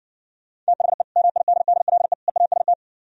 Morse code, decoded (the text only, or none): THE MIDDLE EAST